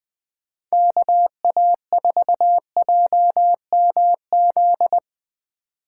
Morse code, heard as a keyed tone, about 20 wpm, 700 Hz. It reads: KA4JMZ
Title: USA Call Signs - 6 Characters - 8x Character Spacing - 20wpm